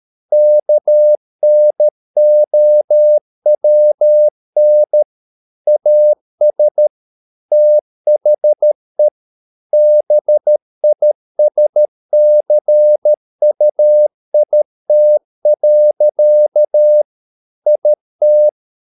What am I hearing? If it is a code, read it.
Morse code, heard as a keyed tone, about 13 wpm, 600 Hz. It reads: KNOWN AS THE BISCUIT. IT